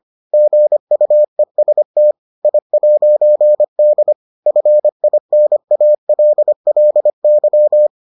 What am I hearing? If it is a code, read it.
Morse code, heard as a keyed tone, about 25 wpm, 600 Hz. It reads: GUEST I'D FINALLY